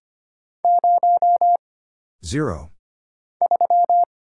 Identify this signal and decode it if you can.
Morse code, heard as a keyed tone, about 25 wpm, 700 Hz. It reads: 0 3
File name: Single Character (Letter:Number:Punctuation:Prosign) - Mind-Melt - 25wpm